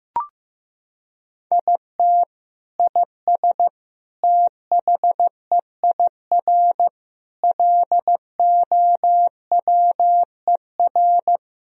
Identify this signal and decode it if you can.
Morse code, heard as a keyed tone, about 15 wpm, 700 Hz. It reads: IT IS THEIR LOWER
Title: Sentences from Top 1000 Words - Review Part 4 - 15wpm